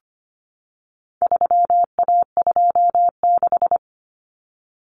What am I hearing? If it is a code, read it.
Morse code, heard as a keyed tone, about 25 wpm, 700 Hz. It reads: 3A26
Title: Common Cut Numbers - 4-Digit 25wpm